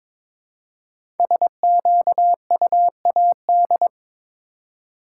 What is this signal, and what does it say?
Morse code, heard as a keyed tone, about 22 wpm, 700 Hz. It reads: SQUAD
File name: practice / 22wpm / 5 Letter Words - First 3 Letters Given - 22wpm